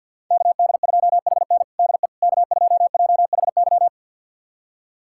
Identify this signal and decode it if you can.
Morse code, heard as a keyed tone, about 47 wpm, 700 Hz. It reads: KD2HN DE K2JHY
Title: USA Callsign DE Callsign - 5 Characters - 50wpm